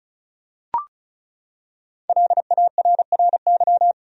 Morse code, heard as a keyed tone, about 36 wpm, 700 Hz. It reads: LARRY
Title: USA Common Names - Max 5 Letters - 35wpm